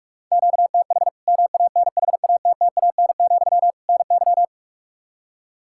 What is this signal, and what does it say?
Morse code, heard as a keyed tone, about 44 wpm, 700 Hz. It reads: QTH MANHATTAN, NY
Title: Practice QSO - Exchange 1 - Beginning - 45wpm